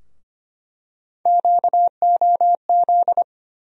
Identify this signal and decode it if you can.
Morse code, heard as a keyed tone, about 25 wpm, 700 Hz. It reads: QOZ
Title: USA Call Sign Postfixes - 3 Letters 25wpm